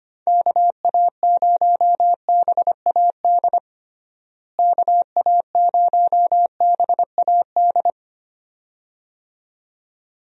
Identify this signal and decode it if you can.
Morse code, heard as a keyed tone, about 25 wpm, 700 Hz. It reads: KA0BAD KA0BAD
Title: USA Call Signs - 6 Characters - Repeated 2x - 25wpm